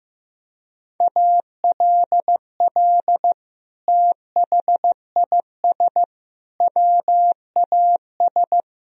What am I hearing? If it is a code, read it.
Morse code, heard as a keyed tone, about 15 wpm, 700 Hz. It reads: ALL THIS WAS
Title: Sentences from Top 1000 Words - Review Part 4 - 15wpm